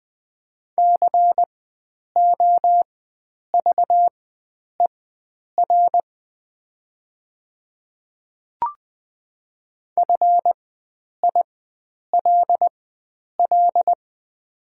Morse code, heard as a keyed tone, about 20 wpm, spaced as for 9 wpm, 700 Hz. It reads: COVER FILL
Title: 4x Character Spacing - Top 500 Words - 20wpm